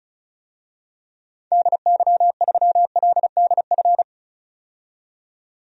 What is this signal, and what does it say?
Morse code, heard as a keyed tone, about 34 wpm, 700 Hz. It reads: DY3LDF
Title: DXCC-100 Call Signs - 6 Characters - 35wpm